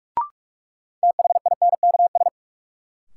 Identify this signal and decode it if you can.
Morse code, heard as a keyed tone, about 46 wpm, 700 Hz. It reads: THINKS
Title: Sentences from Top 1000 Words - Review Part 6 - New Word Drill - 45wpm